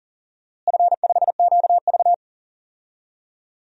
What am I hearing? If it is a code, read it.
Morse code, heard as a keyed tone, about 37 wpm, 700 Hz. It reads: F5QV
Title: DXCC-100 Call Signs - 4 Characters - Repeated 3x - 40wpm